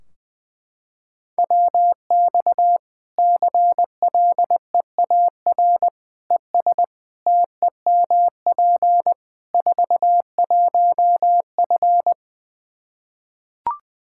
Morse code, heard as a keyed tone, about 20 wpm, 700 Hz. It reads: WX CLEAR ES TEMP 41F
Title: K1USN SST QSO Practice 20wpm